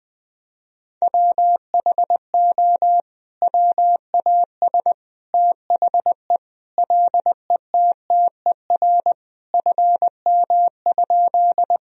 Morse code, heard as a keyed tone, about 21 wpm, 700 Hz. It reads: WHO WAS THE LETTER FM?
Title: Sentences from Top 300 Words and Abbreviations 20wpm